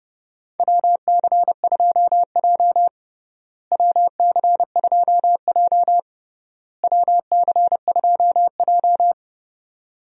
Morse code, heard as a keyed tone, about 32 wpm, 700 Hz. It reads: WC2J WC2J WC2J
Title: USA Call Signs - 4 Characters - Repeated 3x - 30wpm